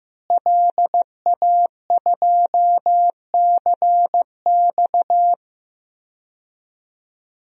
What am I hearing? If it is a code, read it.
Morse code, heard as a keyed tone, about 15 wpm, 700 Hz. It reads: LA2CX
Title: DXCC-100 Call Signs - 5 Characters - Repeated 2x - 15wpm